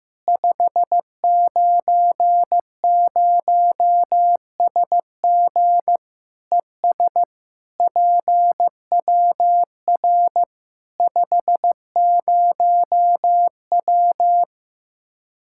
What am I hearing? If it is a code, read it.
Morse code, heard as a keyed tone, about 15 wpm, 700 Hz. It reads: 590SG ES PWR 50W
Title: Practice QSO - Exchange 2 - Info - 15wpm